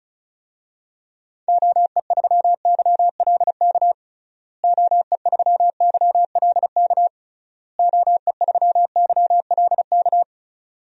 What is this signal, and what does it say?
Morse code, heard as a keyed tone, about 36 wpm, 700 Hz. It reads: OE3YLK OE3YLK OE3YLK
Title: DXCC-100 Call Signs - 6 Characters - Repeated 3x - 35wpm